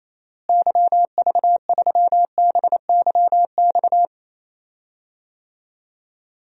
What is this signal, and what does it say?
Morse code, heard as a keyed tone, about 28 wpm, 700 Hz. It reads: YV3BYX
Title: DXCC-100 Call Signs - 6 Characters - Repeated 2x - 28wpm